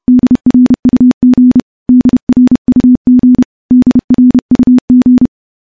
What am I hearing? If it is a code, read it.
Morse code, heard as a keyed tone, about 30 wpm, 250 Hz. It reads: DRUG DRUG DRUG
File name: morse / morse-training-30WPM